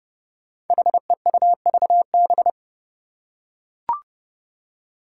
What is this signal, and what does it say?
Morse code, heard as a keyed tone, about 30 wpm, 700 Hz. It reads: HEUVB
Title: Five Challenging Letter-Number Combinations 30wpm